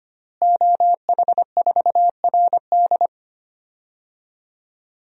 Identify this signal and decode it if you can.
Morse code, heard as a keyed tone, about 25 wpm, 700 Hz. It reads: OH4RD
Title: DXCC-100 Call Signs - 5 Characters - Repeated 3x - 25wpm